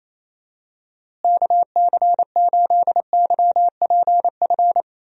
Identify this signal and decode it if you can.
Morse code, heard as a keyed tone, about 27 wpm, 700 Hz. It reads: KC8YPF
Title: USA Call Signs - 6 Characters - Repeated 3x - 28wpm